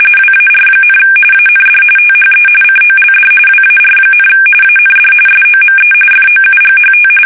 RTTY, about 45.45 baud. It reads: WAR, AND THE COLD WAR, AND SOME FOUGHT
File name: fsk850 / battleshipnj